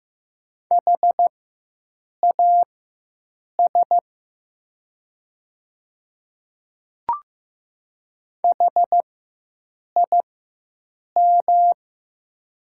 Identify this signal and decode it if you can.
Morse code, heard as a keyed tone, about 15 wpm, spaced as for 7 wpm, 700 Hz. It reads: HAS HIM
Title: 4x Character Spacing - Top 100 Words - 15wpm